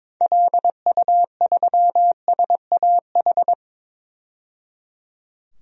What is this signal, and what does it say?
Morse code, heard as a keyed tone, about 22 wpm, 700 Hz. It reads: LU3SAH
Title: DXCC-100 Call Signs - 6 Characters - 22wpm